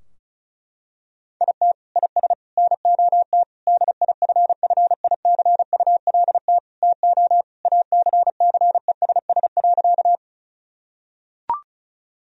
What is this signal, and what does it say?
Morse code, heard as a keyed tone, about 34 wpm, 700 Hz. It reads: IT IS NOT DIFFICULT TO ACCESS.
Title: Sentences from Top 1000 Words - Review Part 4 - 35wpm